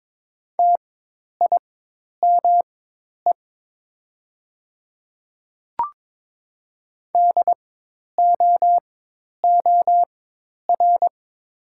Morse code, heard as a keyed tone, about 22 wpm, spaced as for 10 wpm, 700 Hz. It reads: TIME DOOR
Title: 4x Character Spacing - Top 300 Words - 22wpm